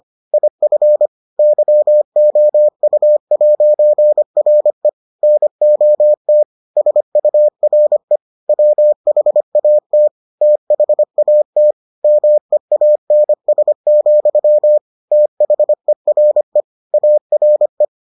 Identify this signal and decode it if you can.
Morse code, heard as a keyed tone, about 25 wpm, 600 Hz. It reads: IF YOU'RE NOT SURE WHAT THAT MEANS, THERE ARE